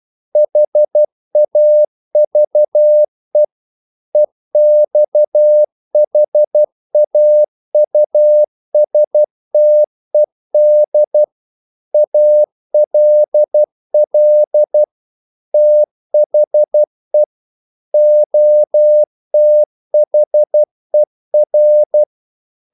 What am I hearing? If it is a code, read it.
Morse code, heard as a keyed tone, about 12 wpm, 600 Hz. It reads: HAVE EXHAUSTED ALL THE OTHER
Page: English quotations in CW (telegraphy - morse)